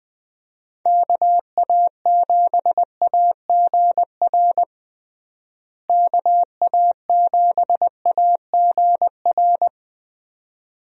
Morse code, heard as a keyed tone, about 20 wpm, 700 Hz. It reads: KA7AGR KA7AGR
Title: USA Call Signs - 6 Characters - Repeated 2x - 20wpm